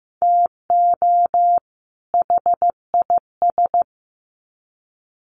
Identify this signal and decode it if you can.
Morse code, heard as a keyed tone, about 15 wpm, 700 Hz. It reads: TO HIS
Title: Sets of 2 Words 15wpm